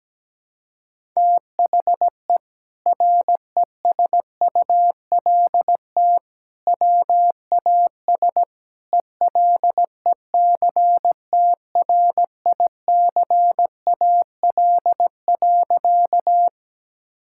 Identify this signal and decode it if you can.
Morse code, heard as a keyed tone, about 17 wpm, 700 Hz. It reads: THE RESULT WAS ELECTRICAL.
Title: Sentences from Top 1000 Words - Review Part 6 - 17wpm